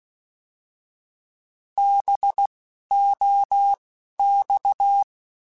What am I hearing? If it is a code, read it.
Morse code, heard as a keyed tone, about 16 wpm, spaced as for 11 wpm, 800 Hz. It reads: BOX
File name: Alphabetical / Alphabetical mp3s / Bs @10